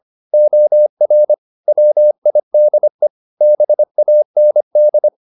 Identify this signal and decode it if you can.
Morse code, heard as a keyed tone, about 25 wpm, 600 Hz. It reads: OR WIDE BAND